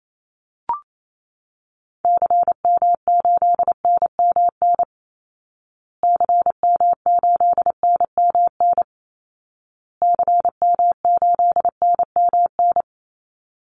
Morse code, heard as a keyed tone, about 28 wpm, 700 Hz. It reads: CM8NMN CM8NMN CM8NMN
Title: Call Signs Repeated 2x - 28wpm